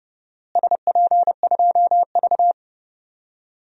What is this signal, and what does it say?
Morse code, heard as a keyed tone, about 32 wpm, 700 Hz. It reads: SP2V
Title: DXCC-100 Call Signs - 4 Characters - Repeated 2x - 30wpm